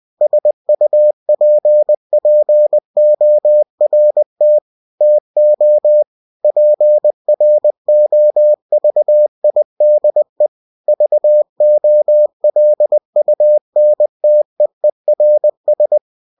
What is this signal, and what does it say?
Morse code, heard as a keyed tone, about 20 wpm, 600 Hz. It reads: SUPPORT TO PROVIDE VOLUNTEERS
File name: cw-books / winnie-the-pooh / winnie-20wpm-page-0120